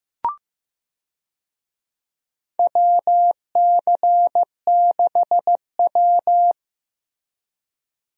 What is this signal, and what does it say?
Morse code, heard as a keyed tone, about 15 wpm, 700 Hz. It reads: WC6W
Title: USA Call Signs - 4 Characters - Repeated 2x - 15wpm